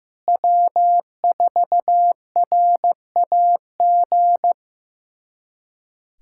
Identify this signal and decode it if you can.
Morse code, heard as a keyed tone, about 15 wpm, 700 Hz. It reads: W4RAG
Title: USA Callsign DE Callsign - 5 Characters - 15wpm